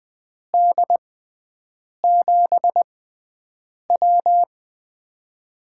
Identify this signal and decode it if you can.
Morse code, heard as a keyed tone, about 20 wpm, spaced as for 7 wpm, 700 Hz. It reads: D7W
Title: USA Call Signs - 4 Characters - 6x Character Spacing - 20wpm